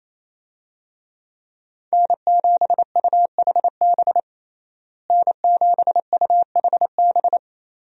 Morse code, heard as a keyed tone, about 28 wpm, 700 Hz. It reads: N7UHB N7UHB
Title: USA Call Signs - 5 Characters - Repeated 3x - 28wpm